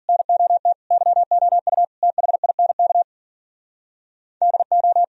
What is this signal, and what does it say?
Morse code, heard as a keyed tone, about 45 wpm, 700 Hz. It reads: NOT YOU THINK DO
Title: Speed-Racing - Sets of 4 Words 40wpm